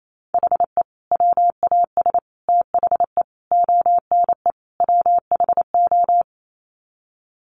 Morse code, heard as a keyed tone, about 28 wpm, 700 Hz. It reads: HE WAS THE ONE WHO
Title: Sets of 5 Words Encore2 28wpm